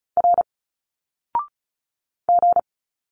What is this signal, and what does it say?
Morse code, heard as a keyed tone, about 36 wpm, 700 Hz. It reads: R G
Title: Single Letters 35wpm